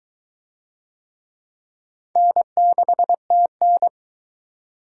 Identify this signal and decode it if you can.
Morse code, heard as a keyed tone, about 20 wpm, 700 Hz. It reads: N6TN